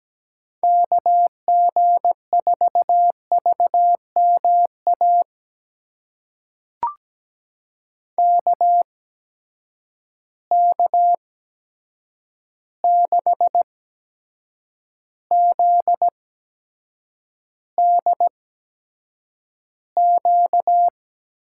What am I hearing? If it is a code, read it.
Morse code, heard as a keyed tone, about 17 wpm, spaced as for 7 wpm, 700 Hz. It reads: KG4VMA KK6ZDQ
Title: USA Call Signs - 6 Characters - 8x Character Spacing - 17wpm